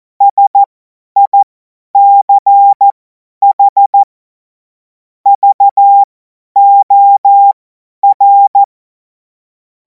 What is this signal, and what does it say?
Morse code, heard as a keyed tone, about 14 wpm, spaced as for 10 wpm, 800 Hz. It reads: SICH VOR
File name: Morsekurs / Witze / Witz 03. März 2026 50 F70